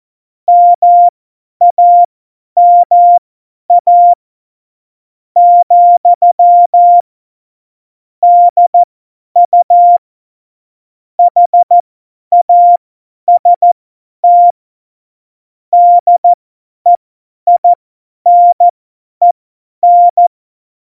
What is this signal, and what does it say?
Morse code, heard as a keyed tone, about 14 wpm, spaced as for 10 wpm, 700 Hz. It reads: MAMA , DU HAST DEINEN